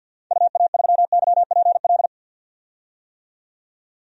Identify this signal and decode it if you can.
Morse code, heard as a keyed tone, about 48 wpm, 700 Hz. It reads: UA3YPL
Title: DXCC-100 Call Signs - 6 Characters - 50wpm